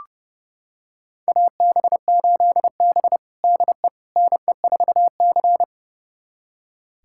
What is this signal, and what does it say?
Morse code, heard as a keyed tone, about 30 wpm, 700 Hz. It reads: AB8B DE NE4C
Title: USA Callsign DE Callsign - 4 Characters - 30wpm